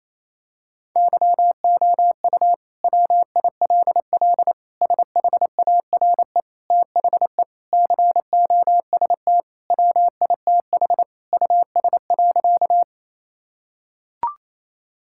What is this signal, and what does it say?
Morse code, heard as a keyed tone, about 28 wpm, 700 Hz. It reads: YOU WILL SHARE THE COST WITH US.
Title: Sentences from Top 1000 Words - Review Part 6 - 28wpm